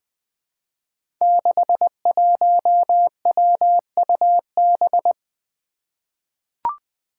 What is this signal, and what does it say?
Morse code, heard as a keyed tone, about 20 wpm, 700 Hz. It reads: 61WUB
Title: Five Challenging Letter-Number Combinations 20wpm